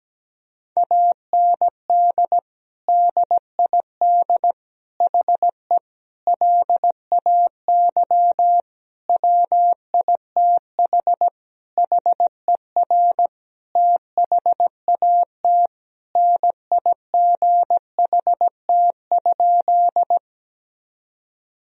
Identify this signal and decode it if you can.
Morse code, heard as a keyed tone, about 17 wpm, 700 Hz. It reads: AND DID HE LAY WITH HER THAT NIGHT?
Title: Sentences from Top 1000 Words - Review Part 2 - 17wpm